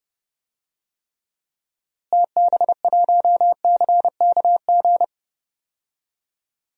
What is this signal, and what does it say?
Morse code, heard as a keyed tone, about 30 wpm, 700 Hz. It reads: TB1CKG
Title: DXCC-100 Call Signs - 6 Characters - Repeated 3x - 30wpm